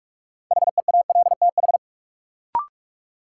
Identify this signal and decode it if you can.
Morse code, heard as a keyed tone, about 46 wpm, 700 Hz. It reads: HEALTH
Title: Sets of 5 Words - New Word Drill - 45wpm